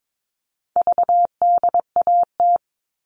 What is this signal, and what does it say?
Morse code, heard as a keyed tone, about 22 wpm, 700 Hz. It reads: VDAT
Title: All Cut Numbers - 4-Digit 22wpm